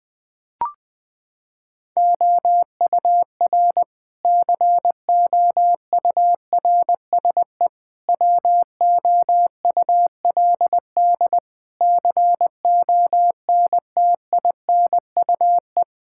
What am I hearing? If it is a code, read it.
Morse code, heard as a keyed tone, about 20 wpm, 700 Hz. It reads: OUR COURSE WOULD CONTINUE
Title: Sentences from Top 1000 Words - Review Part 6 - 20wpm